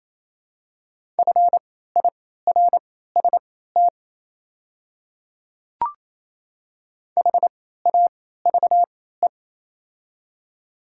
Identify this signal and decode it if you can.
Morse code, heard as a keyed tone, about 28 wpm, spaced as for 16 wpm, 690 Hz. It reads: FIRST HAVE
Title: 3x Character Spacing - Top 200 Words - 28wpm